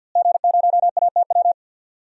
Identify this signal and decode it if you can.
Morse code, heard as a keyed tone, about 48 wpm, 700 Hz. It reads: G0ATW
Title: DXCC-100 Call Signs - 5 Characters - Repeated 2x - 50wpm